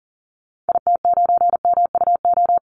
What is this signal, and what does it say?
Morse code, heard as a keyed tone, about 40 wpm, 700 Hz. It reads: IT9MUO
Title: Call Signs Repeated 2x - 40wpm